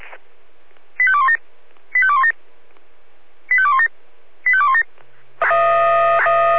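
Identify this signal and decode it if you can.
Five-tone selective call (ZVEI), heard as quick runs of five tones, tones of about 70 ms. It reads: 85217 85217 85217 85217
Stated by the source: Received on an SDR.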